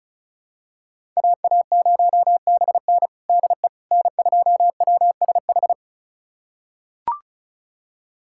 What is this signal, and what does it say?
Morse code, heard as a keyed tone, about 36 wpm, 700 Hz. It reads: AA0BN DE N2WSH
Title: USA Callsign DE Callsign - 5 Characters - 35wpm